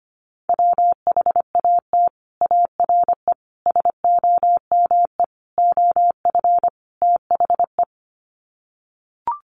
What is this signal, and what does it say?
Morse code, heard as a keyed tone, about 25 wpm, 700 Hz. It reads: WHAT ARE SOME OF THE
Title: Sets of 5 Words Encore 25wpm